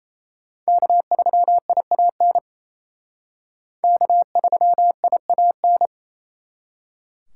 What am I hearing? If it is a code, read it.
Morse code, heard as a keyed tone, about 35 wpm, 700 Hz. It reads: K3IAN K3IAN
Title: Speed Racing - USA 1x3 Call Signs 28wpm